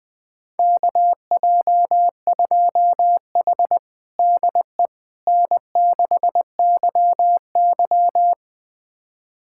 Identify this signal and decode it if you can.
Morse code, heard as a keyed tone, about 20 wpm, 700 Hz. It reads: KJ2H DE N6YY
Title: USA Callsign DE Callsign - 4 Characters - 20wpm